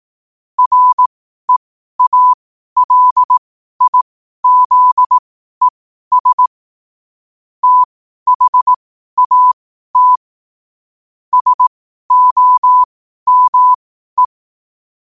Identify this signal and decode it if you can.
Morse code, heard as a keyed tone, about 18 wpm, spaced as for 12 wpm, 1000 Hz. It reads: REALIZES THAT SOME
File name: qotdfiles / 12wpm